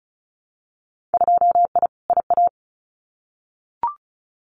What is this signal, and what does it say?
Morse code, heard as a keyed tone, about 36 wpm, 700 Hz. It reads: 2I IA
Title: Winter Field Day Exchanges Encore 35wpm